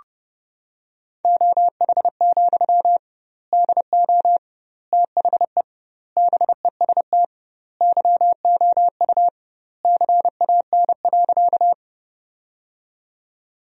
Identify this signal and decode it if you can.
Morse code, heard as a keyed tone, about 32 wpm, 700 Hz. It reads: OH, DO THE BEST YOU CAN.
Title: 2x Word-Spacing - Sentences from Top 500 Words - 30wpm